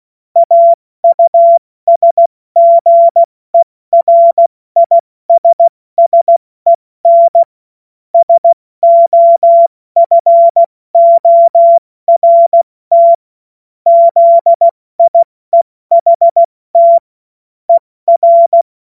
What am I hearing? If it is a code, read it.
Morse code, heard as a keyed tone, about 16 wpm, 680 Hz. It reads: AUSGERISSEN SOFORT ZIEHT ER
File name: Morsekurs / Witze / Witz 31.Oktober 2023 70 F80